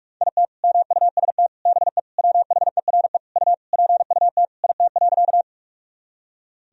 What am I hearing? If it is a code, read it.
Morse code, heard as a keyed tone, about 46 wpm, 700 Hz. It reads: IT MUST BE WHERE U PUT IT.
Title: Sentences from Top 200 Words and Abbreviations 45wpm